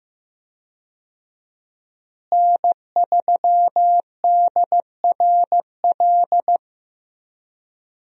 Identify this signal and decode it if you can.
Morse code, heard as a keyed tone, about 15 wpm, 700 Hz. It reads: N3DRL
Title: USA Call Signs - 5 Characters - Repeated 2x - 15wpm